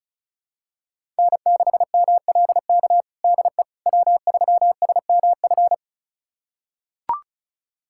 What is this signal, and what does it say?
Morse code, heard as a keyed tone, about 36 wpm, 700 Hz. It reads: N6MLK DE W3SMF
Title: USA Callsign DE Callsign - 5 Characters - 35wpm